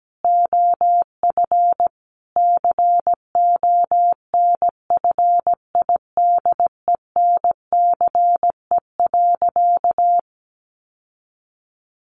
Morse code, heard as a keyed tone, about 17 wpm, 700 Hz. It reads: OF CONFIDENCE.
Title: Sentences from Top 1500 Words 17wpm